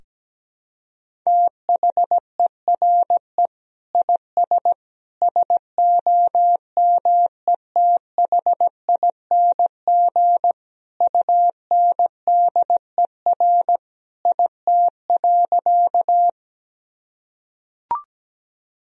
Morse code, heard as a keyed tone, about 17 wpm, 700 Hz. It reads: THERE IS SOMETHING UNDER IT.
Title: Sentences from Top 1000 Words - Review Part 1 - 17wpm